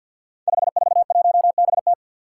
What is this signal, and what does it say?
Morse code, heard as a keyed tone, about 47 wpm, 700 Hz. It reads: 541BT
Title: Five Challenging Letter-Number Combinations 50wpm